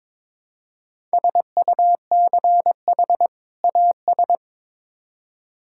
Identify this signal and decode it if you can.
Morse code, heard as a keyed tone, about 22 wpm, 700 Hz. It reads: SUCH AS
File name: Build A Phrase - 2-Word Phrases - 22wpm